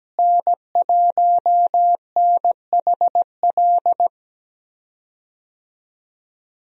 Morse code, heard as a keyed tone, about 17 wpm, 700 Hz. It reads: N1NHL
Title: USA Call Signs - 5 Characters - Repeated 2x - 17wpm